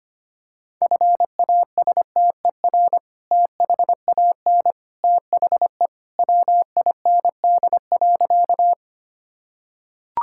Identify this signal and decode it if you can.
Morse code, heard as a keyed tone, about 25 wpm, 700 Hz. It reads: FASTER THAN THE WIND.
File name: practice / 25wpm / Sentences with 2-Word Phrases - 25wpm